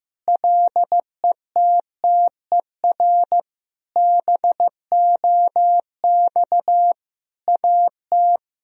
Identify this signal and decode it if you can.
Morse code, heard as a keyed tone, about 15 wpm, 700 Hz. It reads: LETTER BOX AT
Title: Sentences from Top 1000 Words - Review Part 2 - 15wpm